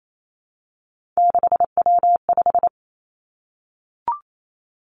Morse code, heard as a keyed tone, about 28 wpm, 700 Hz. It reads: 6W5
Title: Three Challenging Letter-Number Combinations 28wpm